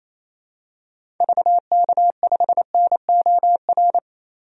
Morse code, heard as a keyed tone, about 28 wpm, 700 Hz. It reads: VK5NOR
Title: DXCC-100 Call Signs - 6 Characters - Repeated 3x - 28wpm